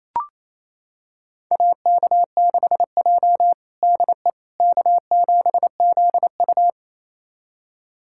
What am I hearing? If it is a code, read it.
Morse code, heard as a keyed tone, about 28 wpm, 700 Hz. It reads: AK6J DE K7ZU
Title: USA Callsign DE Callsign - 4 Characters - 28wpm